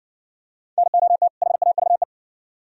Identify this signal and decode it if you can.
Morse code, heard as a keyed tone, about 60 wpm, 700 Hz. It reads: NOT HAVE